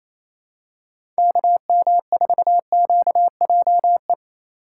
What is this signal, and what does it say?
Morse code, heard as a keyed tone, about 28 wpm, 700 Hz. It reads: KM4QJE
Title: USA Call Signs - 6 Characters - 6x Character Spacing - 28wpm